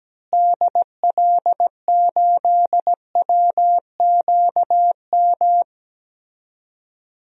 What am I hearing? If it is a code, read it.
Morse code, heard as a keyed tone, about 17 wpm, 700 Hz. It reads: DL8WQM
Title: DXCC-100 Call Signs - 6 Characters - Repeated 2x - 17wpm